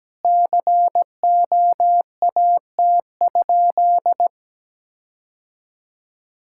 Morse code, heard as a keyed tone, about 17 wpm, 700 Hz. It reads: COAT?
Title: Sentences from Top 1600 Words 17wpm